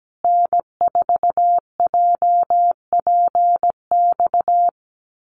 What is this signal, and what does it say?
Morse code, heard as a keyed tone, about 17 wpm, 700 Hz. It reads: N4JPX
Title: USA Call Signs Repeated 3x 17wpm